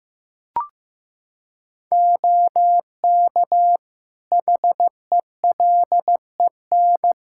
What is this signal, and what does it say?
Morse code, heard as a keyed tone, about 15 wpm, 700 Hz. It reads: OK HELEN
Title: Practice QSO - Exchange 3 - More Info - 15wpm